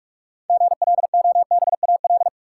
Morse code, heard as a keyed tone, about 44 wpm, 700 Hz. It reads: GLOBAL
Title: Sentences from Top 1200 Words - New Word Drill - 45wpm